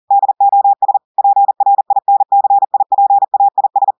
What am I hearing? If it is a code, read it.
Morse code, heard as a keyed tone, about 40 wpm, 800 Hz. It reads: DOS PRINCIPAIS
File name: audios / portugais / portugais texte en clair 1 vitesse 40 mots minutes